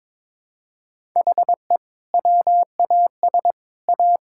Morse code, heard as a keyed tone, about 22 wpm, 700 Hz. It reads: HE WAS A
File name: practice / 22wpm / Sets of 6 Words - 22wpm